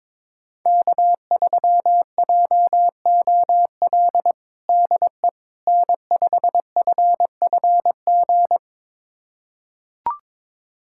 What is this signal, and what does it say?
Morse code, heard as a keyed tone, about 22 wpm, 700 Hz. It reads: K3JOL DE N5FFG
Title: USA Callsign DE Callsign - 5 Characters - 22wpm